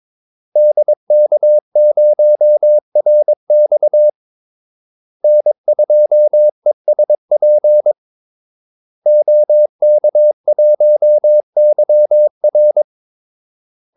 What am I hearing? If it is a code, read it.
Morse code, heard as a keyed tone, about 22 wpm, 600 Hz. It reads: DK0RX N2ESP OK1YR
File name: Callsigns-22wpm